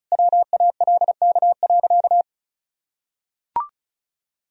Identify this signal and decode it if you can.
Morse code, heard as a keyed tone, about 36 wpm, 700 Hz. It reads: WALK.
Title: Sentences from Top 1000 Words - Review Part 1 - 35wpm